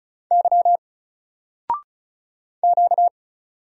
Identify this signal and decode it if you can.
Morse code, heard as a keyed tone, about 36 wpm, 700 Hz. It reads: Y Q
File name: Single-Letter-Number-35wpm